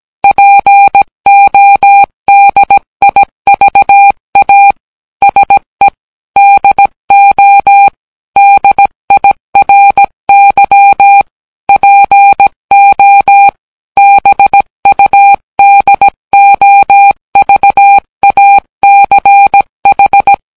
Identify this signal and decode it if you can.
Morse code, heard as a keyed tone, about 17 wpm, 800 Hz. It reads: PODIVA SE DO DIRY PO BUDOVACH